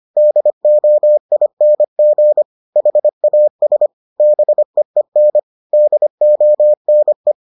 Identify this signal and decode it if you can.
Morse code, heard as a keyed tone, about 25 wpm, 600 Hz. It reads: DOING HAS BEEN DONE